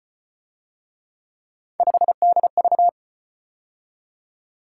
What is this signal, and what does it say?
Morse code, heard as a keyed tone, about 34 wpm, 700 Hz. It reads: 5DV